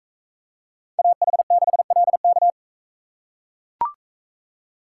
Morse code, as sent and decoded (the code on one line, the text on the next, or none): .- .... -.... .-.. -.-
AH6LK